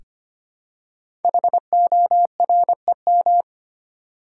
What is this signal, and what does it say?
Morse code, heard as a keyed tone, about 25 wpm, 700 Hz. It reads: HOREM
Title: Code Groups - 5 Letters - 25wpm